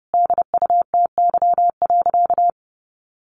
Morse code, heard as a keyed tone, about 30 wpm, 700 Hz. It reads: DUTY.
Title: Sentences from Top 1500 Words 30wpm